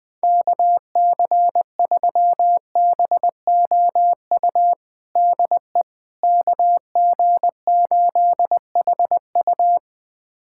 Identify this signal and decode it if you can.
Morse code, heard as a keyed tone, about 20 wpm, 700 Hz. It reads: KC3BOU DE KG8HU
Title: USA Callsign DE Callsign 20wpm